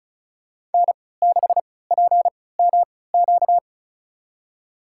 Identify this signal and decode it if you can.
Morse code, heard as a keyed tone, about 36 wpm, spaced as for 20 wpm, 700 Hz. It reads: N6PMQ
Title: USA Call Signs - 3x Character Spacing - 35wpm